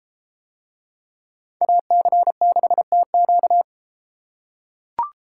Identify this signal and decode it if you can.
Morse code, heard as a keyed tone, about 33 wpm, 700 Hz. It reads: AC6TQ